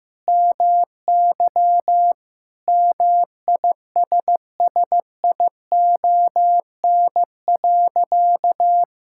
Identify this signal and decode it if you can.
Morse code, heard as a keyed tone, about 15 wpm, 700 Hz. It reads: MY MISSION.
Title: Sentences from Top 1600 Words 15wpm